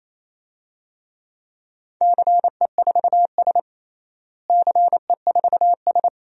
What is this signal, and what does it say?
Morse code, heard as a keyed tone, about 28 wpm, 700 Hz. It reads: CE4S CE4S
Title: DXCC-100 Call Signs - 4 Characters - Repeated 2x - 28wpm